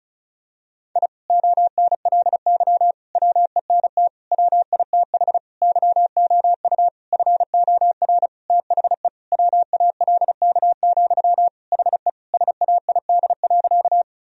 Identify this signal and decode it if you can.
Morse code, heard as a keyed tone, about 34 wpm, 700 Hz. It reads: I ONLY WENT WITH YOU FOR THE WALK, HE SAID.
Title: Sentences from Top 1000 Words - Review Part 1 - 35wpm